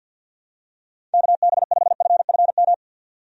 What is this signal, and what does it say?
Morse code, heard as a keyed tone, about 50 wpm, 700 Hz. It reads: KB5FFM
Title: USA Call Signs - 6 Characters - 50wpm